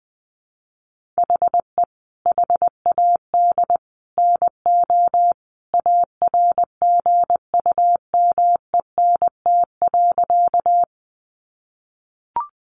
Morse code, heard as a keyed tone, about 20 wpm, 700 Hz. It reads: HE HAD NO ARGUMENT.
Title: Sentences from Top 1400 Words 20wpm